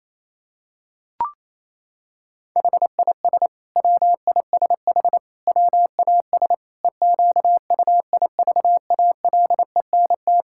Morse code, heard as a keyed tone, about 28 wpm, 700 Hz. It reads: HIS WISH WAS EQUIVALENT
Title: Sentences from Top 1600 Words 28wpm